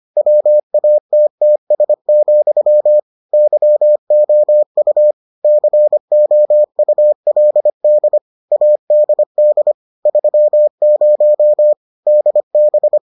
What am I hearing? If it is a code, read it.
Morse code, heard as a keyed tone, about 25 wpm, 600 Hz. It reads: WATTS, YOU COULD ADD 30 DB